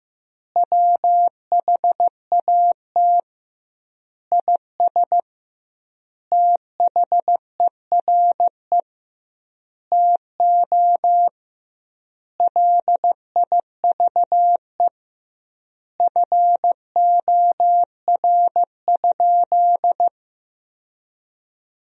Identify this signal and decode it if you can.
Morse code, heard as a keyed tone, about 15 wpm, 700 Hz. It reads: WHAT IS THERE TO LIVE FOR?
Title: 2x Word-Spacing - Sentences from Top 200 Words - 15wpm